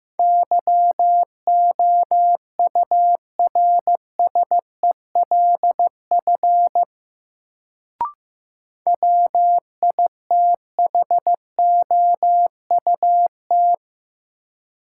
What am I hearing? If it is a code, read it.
Morse code, heard as a keyed tone, about 15 wpm, 700 Hz. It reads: YOURSELF WITHOUT
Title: Sentences from Top 600 Words - New Word Drill - 15wpm